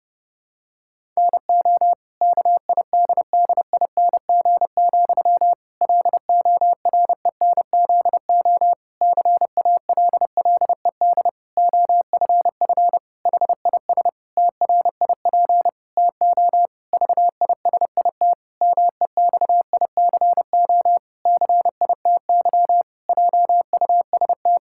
Morse code, heard as a keyed tone, about 32 wpm, 700 Hz. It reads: NO KIDDING, LORENZO CALLED OFF HIS TRIP TO VISIT MEXICO CITY JUST
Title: Pangram Challenge v2 - 30wpm